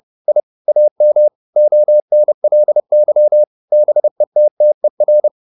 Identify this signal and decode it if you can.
Morse code, heard as a keyed tone, about 30 wpm, 600 Hz. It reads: I AM ONLY BETTER